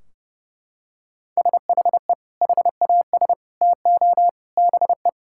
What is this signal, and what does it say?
Morse code, heard as a keyed tone, about 30 wpm, 700 Hz. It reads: SHE HAS TO BE
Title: Sentences from Top 1600 Words 30wpm